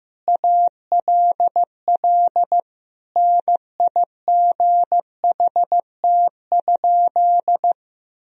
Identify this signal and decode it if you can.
Morse code, heard as a keyed tone, about 15 wpm, 700 Hz. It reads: ALL NIGHT?
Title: Sentences from Top 1000 Words - Review Part 4 - 15wpm